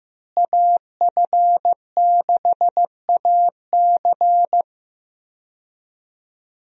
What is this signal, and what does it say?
Morse code, heard as a keyed tone, about 15 wpm, 700 Hz. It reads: AF6AC
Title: USA Call Signs - 5 Characters - Repeated 3x - 15wpm